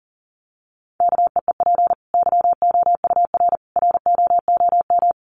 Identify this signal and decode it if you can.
Morse code, heard as a keyed tone, about 40 wpm, 700 Hz. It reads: KEEP YOUR ROOM
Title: Sentences from Top 1100 Words 40wpm